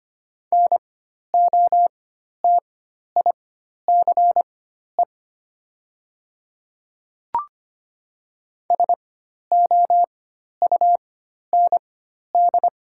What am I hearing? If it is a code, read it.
Morse code, heard as a keyed tone, about 25 wpm, spaced as for 12 wpm, 690 Hz. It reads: NOTICE SOUND
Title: 4x Character Spacing - Top 500 Words - 25wpm